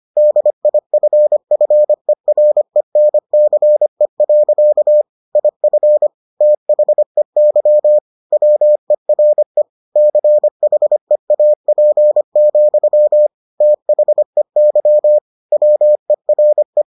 Morse code, heard as a keyed tone, about 25 wpm, 600 Hz. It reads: DIFFERENCE. IF THEY WERE CHEAP, THEY WERE